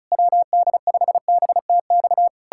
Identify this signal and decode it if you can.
Morse code, heard as a keyed tone, about 40 wpm, 700 Hz. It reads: WD5BTX